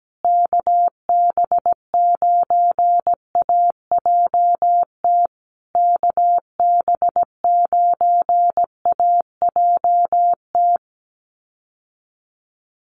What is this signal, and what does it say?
Morse code, heard as a keyed tone, about 17 wpm, 700 Hz. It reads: KB9AJT KB9AJT
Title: USA Call Signs Repeated 3x 17wpm